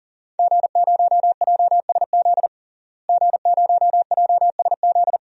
Morse code, heard as a keyed tone, about 40 wpm, 700 Hz. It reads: G0JSZ G0JSZ
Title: DXCC-100 Call Signs - 5 Characters - Repeated 2x - 40wpm